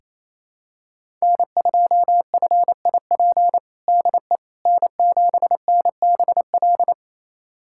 Morse code, heard as a keyed tone, about 28 wpm, 700 Hz. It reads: N2FIP DE N7NBL
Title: USA Callsign DE Callsign - 5 Characters - 28wpm